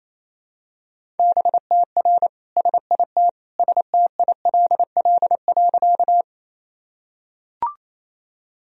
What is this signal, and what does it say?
Morse code, heard as a keyed tone, about 28 wpm, 700 Hz. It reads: BTR SIT STILL.
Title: Sentences from Top 500 Words and Abbreviations 28wpm